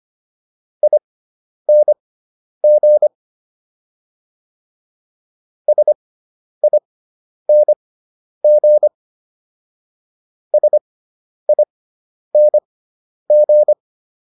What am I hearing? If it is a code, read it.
Morse code, heard as a keyed tone, about 25 wpm, spaced as for 10 wpm, 600 Hz. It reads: ING SING SING